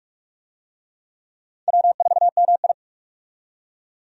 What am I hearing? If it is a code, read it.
Morse code, heard as a keyed tone, about 46 wpm, 700 Hz. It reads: W4MI
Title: USA Call Signs - 4 Characters - 45wpm